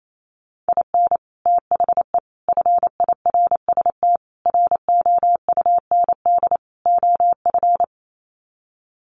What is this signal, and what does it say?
Morse code, heard as a keyed tone, about 28 wpm, 700 Hz. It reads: IN THE FIRST ROUND OF
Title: Sets of 5 Words Encore 28wpm